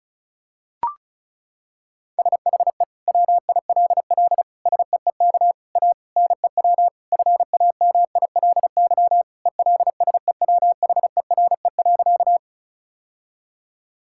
Morse code, heard as a keyed tone, about 36 wpm, 700 Hz. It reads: SHE WILL SEEK A NEW FAMILY ELSEWHERE.
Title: Sentences from Top 1600 Words 35wpm